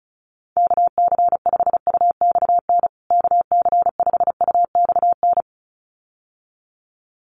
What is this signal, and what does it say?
Morse code, heard as a keyed tone, about 34 wpm, 700 Hz. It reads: KC5UXN KC5UXN
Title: USA Call Signs Repeated 2x 35wpm